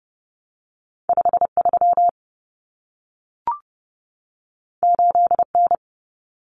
Morse code, heard as a keyed tone, about 30 wpm, 700 Hz. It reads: 53 8N
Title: Common Cut Numbers - 2-Digit 30wpm